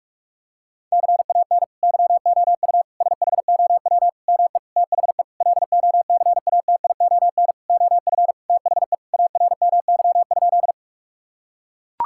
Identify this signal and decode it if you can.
Morse code, heard as a keyed tone, about 41 wpm, 700 Hz. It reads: CAN YOU SHOW ME THE LOCATION OF THE ARMY?
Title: Sentences from Top 1000 Words - Review Part 6 - 45wpm